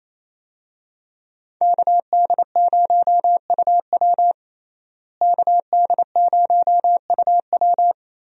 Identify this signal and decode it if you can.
Morse code, heard as a keyed tone, about 28 wpm, 700 Hz. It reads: KD0UW KD0UW
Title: USA Call Signs - 5 Characters - Repeated 2x - 28wpm